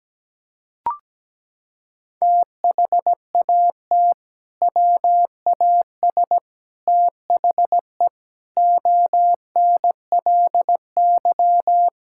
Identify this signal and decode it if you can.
Morse code, heard as a keyed tone, about 17 wpm, 700 Hz. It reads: THAT WAS THE ONLY
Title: Sentences from Top 1600 Words 17wpm